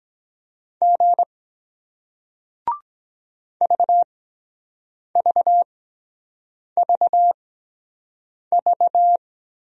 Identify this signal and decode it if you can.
Morse code, heard as a keyed tone, about 23 wpm, spaced as for 7 wpm, 700 Hz. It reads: G VVVV